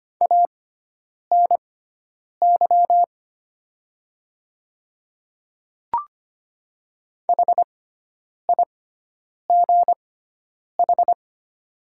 Morse code, heard as a keyed tone, about 25 wpm, spaced as for 9 wpm, 700 Hz. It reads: ANY HIGH
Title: 6x Character Spacing - Top 200 Words - 25wpm